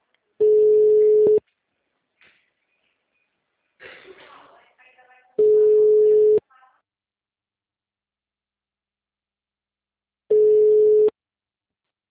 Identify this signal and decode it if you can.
Ringback tone (Europe, 425 Hz), with speech.